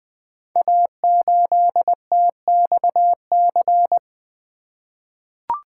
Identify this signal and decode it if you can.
Morse code, heard as a keyed tone, about 20 wpm, 700 Hz. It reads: A8TXC
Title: Code Groups - 5 Letter:Numbers - 20wpm